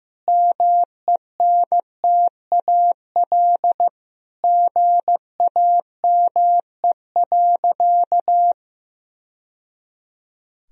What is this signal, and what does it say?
Morse code, heard as a keyed tone, about 16 wpm, 700 Hz. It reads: MENTAL GAME.
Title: Sentences from Top 1000 Words - Review Part 4 - 15wpm